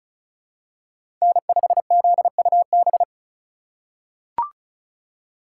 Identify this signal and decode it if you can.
Morse code, heard as a keyed tone, about 34 wpm, 700 Hz. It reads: N5ZUB
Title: USA Call Signs - 5 Characters - 4x Character Spacing - 35wpm